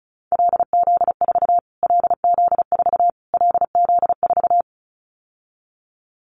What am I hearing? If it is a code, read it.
Morse code, heard as a keyed tone, about 34 wpm, 700 Hz. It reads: LZ4 LZ4 LZ4
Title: Callsign Prefixes for Top 25 DXCC - Repeated 3x - 35wpm